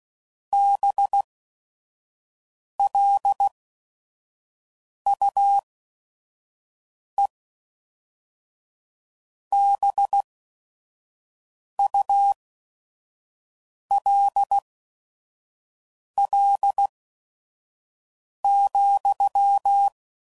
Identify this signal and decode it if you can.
Morse code, heard as a keyed tone, about 16 wpm, spaced as for 5 wpm, 800 Hz. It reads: BLUEBULL,